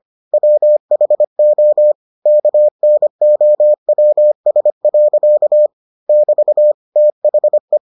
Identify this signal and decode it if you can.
Morse code, heard as a keyed tone, about 25 wpm, 600 Hz. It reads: WHO KNOWS. <BT> THE